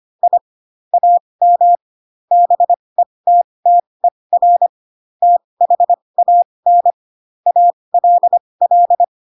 Morse code, heard as a keyed tone, about 25 wpm, spaced as for 20 wpm, 700 Hz. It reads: I AM BETTER THAN ALL
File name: Pinocchio / Pinocchio ch8 15 25 ik1hge